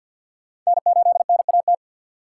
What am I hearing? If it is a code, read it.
Morse code, heard as a keyed tone, about 53 wpm, 700 Hz. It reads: N8NAT
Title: USA Call Signs - 5 Characters - 4x Character Spacing - 50wpm